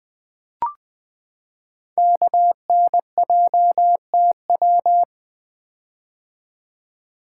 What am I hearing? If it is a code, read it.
Morse code, heard as a keyed tone, about 20 wpm, 700 Hz. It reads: KNJTW
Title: Code Groups - 5 Letter:Numbers - 20wpm